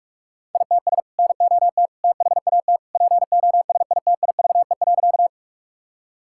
Morse code, heard as a keyed tone, about 47 wpm, 700 Hz. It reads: ITS NOT THAT POSITIVE.
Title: Sentences from Top 1000 Words - Review Part 5 - 45wpm